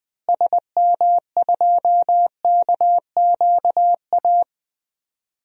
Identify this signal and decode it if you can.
Morse code, heard as a keyed tone, about 20 wpm, 700 Hz. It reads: SM2KQA
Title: DXCC-100 Call Signs - 6 Characters - Repeated 3x - 20wpm